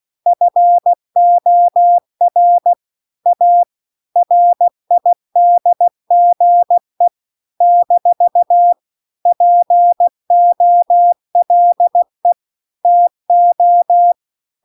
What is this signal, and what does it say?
Morse code, heard as a keyed tone, about 16 wpm, 700 Hz. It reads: FOR A RIDGE - POLE TO